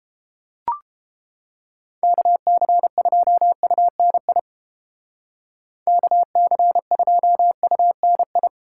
Morse code, heard as a keyed tone, about 30 wpm, 700 Hz. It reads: KC2UNI KC2UNI